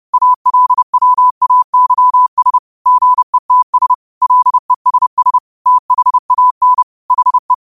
Morse code, heard as a keyed tone, about 30 wpm, 1000 Hz. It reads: ALWAYS GETS LESS THAN HE